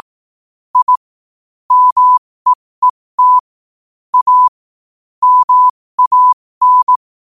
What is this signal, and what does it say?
Morse code, heard as a keyed tone, about 18 wpm, spaced as for 15 wpm, 1000 Hz. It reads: I MEET A MAN
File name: qotdfiles / 15wpm